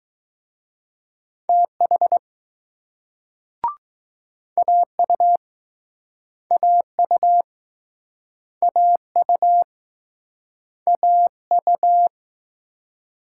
Morse code, heard as a keyed tone, about 21 wpm, 700 Hz. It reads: TH AU AU AU AU